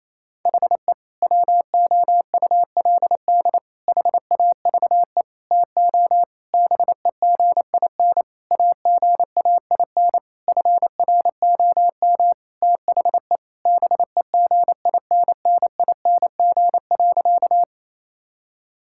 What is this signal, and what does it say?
Morse code, heard as a keyed tone, about 28 wpm, 700 Hz. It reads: HE WOULD HAVE TO BEGIN AGAIN FROM THE BEGINNING.
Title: Sentences from Top 1000 Words - Review Part 5 - 28wpm